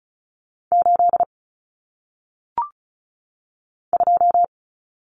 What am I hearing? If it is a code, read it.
Morse code, heard as a keyed tone, about 36 wpm, 700 Hz. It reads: 8 2
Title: Common Cut Numbers - 1-Digit 35wpm